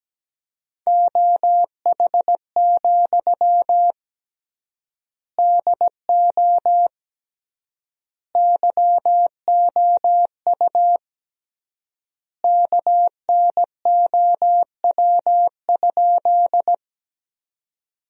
Morse code, heard as a keyed tone, about 17 wpm, 700 Hz. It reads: OH, DO YOU KNOW?
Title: 3x Word-Spacing - Sentences from Top 500 Words - 17wpm